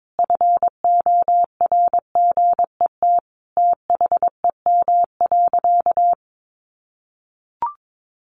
Morse code, heard as a keyed tone, about 22 wpm, 700 Hz. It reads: FORGET THEM.
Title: Sentences from Top 1200 Words 22wpm